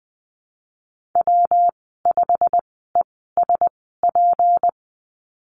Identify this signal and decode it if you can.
Morse code, heard as a keyed tone, about 20 wpm, spaced as for 15 wpm, 700 Hz. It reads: W5ESP
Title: 186 Lesson 30g - Letter b - Call Signs 20wpm